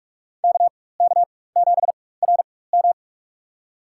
Morse code, heard as a keyed tone, about 44 wpm, spaced as for 21 wpm, 700 Hz. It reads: KK7RM
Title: USA Call Signs - 5 Characters - 4x Character Spacing - 45wpm